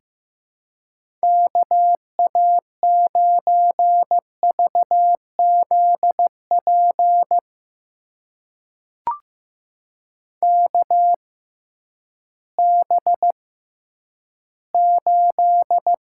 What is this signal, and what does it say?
Morse code, heard as a keyed tone, about 15 wpm, spaced as for 10 wpm, 700 Hz. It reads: KA9VZP KB8
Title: USA Call Signs - 6 Characters - 6x Character Spacing - 15wpm